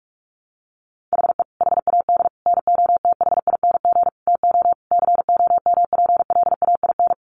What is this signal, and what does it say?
Morse code, heard as a keyed tone, about 44 wpm, 700 Hz. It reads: HE HAD NOTHING TO COMPLAIN
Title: Sentences from Top 1300 Words 45wpm